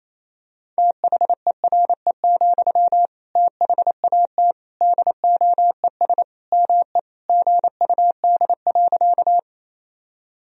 Morse code, heard as a keyed tone, about 28 wpm, 700 Hz. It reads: THERE, THAT DOES ME GUD.
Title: Sentences from Top 200 Words and Abbreviations 28wpm